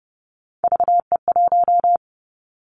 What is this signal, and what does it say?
Morse code, heard as a keyed tone, about 30 wpm, 700 Hz. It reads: VE1
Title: Callsign Prefixes for Top 100 DXCC - 30wpm